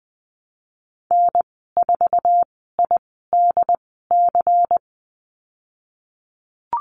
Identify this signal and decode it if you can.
Morse code, heard as a keyed tone, about 20 wpm, spaced as for 15 wpm, 700 Hz. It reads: N4IDC
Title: 128 Lesson 21g - Number 6 - Call Signs 20wpm